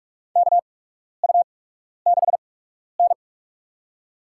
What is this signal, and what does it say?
Morse code, heard as a keyed tone, about 46 wpm, spaced as for 12 wpm, 700 Hz. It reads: KU6N
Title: USA Call Signs - 4 Characters - 8x Character Spacing - 45wpm